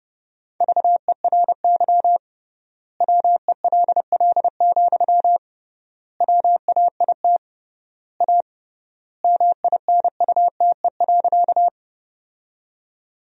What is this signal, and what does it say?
Morse code, heard as a keyed tone, about 32 wpm, 700 Hz. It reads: VERY WELL, WAIT A MINUTE.
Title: 3x Word-Spacing - Sentences from Top 500 Words - 30wpm